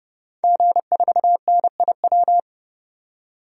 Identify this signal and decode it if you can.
Morse code, heard as a keyed tone, about 32 wpm, 700 Hz. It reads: G4NIW
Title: DXCC-100 Call Signs - 5 Characters - 30wpm